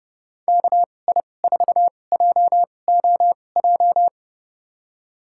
Morse code, heard as a keyed tone, about 29 wpm, spaced as for 21 wpm, 700 Hz. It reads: KI4JOJ
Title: USA Call Signs - 2x Character Spacing - 30wpm